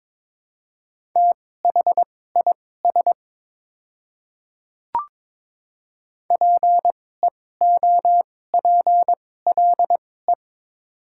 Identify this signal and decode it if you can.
Morse code, heard as a keyed tone, about 22 wpm, spaced as for 16 wpm, 700 Hz. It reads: THIS PEOPLE
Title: 2x Character Spacing - Top 500 Words - 22wpm